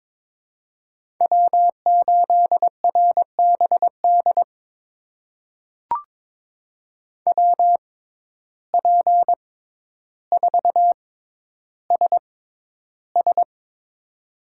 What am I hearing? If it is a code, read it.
Morse code, heard as a keyed tone, about 22 wpm, spaced as for 11 wpm, 700 Hz. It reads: W8RBD WP4SS
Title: USA Call Signs - 5 Characters - 6x Character Spacing - 22wpm